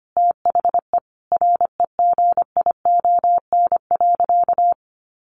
Morse code, heard as a keyed tone, about 25 wpm, 700 Hz. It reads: THE REGION.
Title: Sentences from Top 1500 Words 25wpm